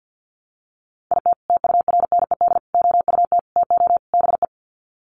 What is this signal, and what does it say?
Morse code, heard as a keyed tone, about 49 wpm, 700 Hz. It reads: IT TURNED OUT TO BE
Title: Sets of 5 Words 50wpm